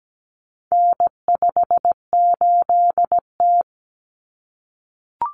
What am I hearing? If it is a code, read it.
Morse code, heard as a keyed tone, about 17 wpm, 700 Hz. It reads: N58T
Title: Common Cut Numbers - 4-Digit 17wpm